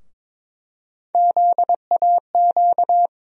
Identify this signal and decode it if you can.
Morse code, heard as a keyed tone, about 22 wpm, 700 Hz. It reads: ZAQ
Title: USA Call Sign Postfixes - 3 Letters 22wpm